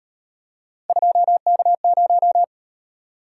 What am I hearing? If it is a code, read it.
Morse code, heard as a keyed tone, about 38 wpm, 700 Hz. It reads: 2K0